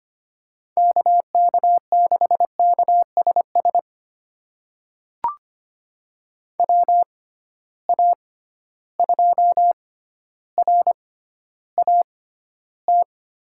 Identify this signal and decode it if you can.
Morse code, heard as a keyed tone, about 25 wpm, spaced as for 13 wpm, 700 Hz. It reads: KK6KSS WA2RAT
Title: USA Call Signs - 6 Characters - 6x Character Spacing - 25wpm